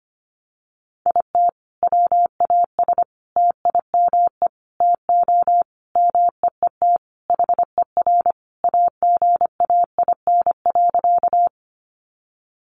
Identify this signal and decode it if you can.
Morse code, heard as a keyed tone, about 25 wpm, 700 Hz. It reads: IT WAS TIME TO MEET HER AGAIN.
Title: Sentences from Top 1100 Words 25wpm